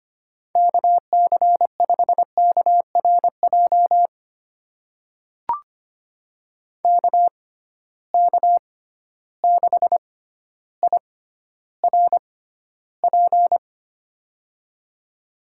Morse code, heard as a keyed tone, about 25 wpm, spaced as for 13 wpm, 700 Hz. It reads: KC5KRJ KK6IRP
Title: USA Call Signs - 6 Characters - 6x Character Spacing - 25wpm